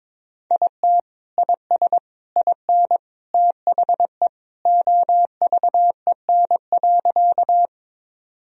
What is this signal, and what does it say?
Morse code, heard as a keyed tone, about 22 wpm, 700 Hz. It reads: IT IS IN THE OVEN.
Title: Sentences from Top 1000 Words - Review Part 4 - 22wpm